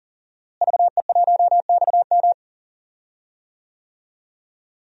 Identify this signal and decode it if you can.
Morse code, heard as a keyed tone, about 40 wpm, 700 Hz. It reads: VE1XM
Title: DXCC-100 Call Signs - 5 Characters - Repeated 2x - 40wpm